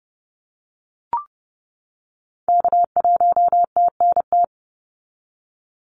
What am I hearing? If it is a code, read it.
Morse code, heard as a keyed tone, about 30 wpm, 700 Hz. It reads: K1TNT
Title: NA SOTA Activator Callsigns 30wpm